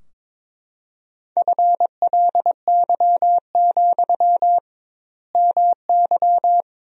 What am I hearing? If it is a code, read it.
Morse code, heard as a keyed tone, about 22 wpm, 700 Hz. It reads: FLY, MY
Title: 2x Word-Spacing - Sentences from Top 500 Words - 22wpm